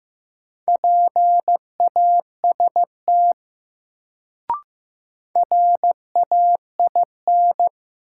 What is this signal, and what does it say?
Morse code, heard as a keyed tone, about 15 wpm, 700 Hz. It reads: PAST RAIN
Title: Sentences from Top 600 Words - New Word Drill - 15wpm